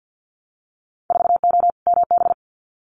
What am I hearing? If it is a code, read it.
Morse code, heard as a keyed tone, about 50 wpm, 700 Hz. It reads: 4O MB
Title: Winter Field Day Exchanges Encore 50wpm